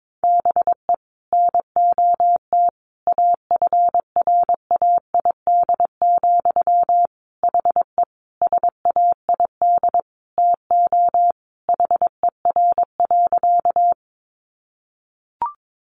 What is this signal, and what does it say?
Morse code, heard as a keyed tone, about 22 wpm, 700 Hz. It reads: BE NOT AFRAID, HE SAID TO HER.
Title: Sentences from Top 1500 Words 22wpm